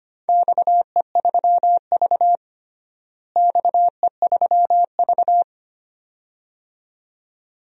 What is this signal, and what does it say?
Morse code, heard as a keyed tone, about 25 wpm, 700 Hz. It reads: XE3V XE3V
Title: DXCC-100 Call Signs - 4 Characters - Repeated 2x - 25wpm